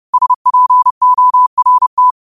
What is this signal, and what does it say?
Morse code, heard as a keyed tone, about 30 wpm, 1000 Hz. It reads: SPORT